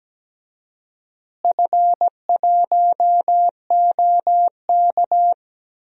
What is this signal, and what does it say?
Morse code, heard as a keyed tone, about 17 wpm, 700 Hz. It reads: F1OK
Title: DXCC-100 Call Signs - 4 Characters - Repeated 3x - 17wpm